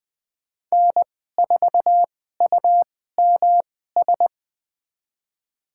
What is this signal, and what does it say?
Morse code, heard as a keyed tone, about 20 wpm, spaced as for 15 wpm, 700 Hz. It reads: N4UMS
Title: USA Call Signs - 5 Characters - 2x Character Spacing - 20wpm